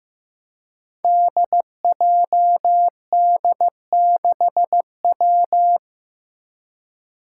Morse code, heard as a keyed tone, about 15 wpm, 700 Hz. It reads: DJD6W
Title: Five Challenging Letter-Number Combinations 15wpm